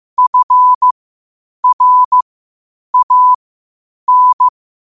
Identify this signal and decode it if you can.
Morse code, heard as a keyed tone, about 15 wpm, spaced as for 8 wpm, 1000 Hz. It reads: FRAN
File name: qotdfiles / 8wpm